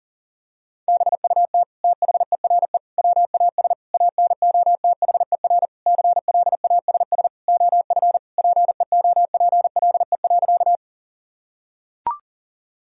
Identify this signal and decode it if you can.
Morse code, heard as a keyed tone, about 40 wpm, 700 Hz. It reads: BUT THERE WAS ANOTHER CLASS OF PEOPLE.
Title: Sentences from Top 1000 Words - Review Part 2 - 40wpm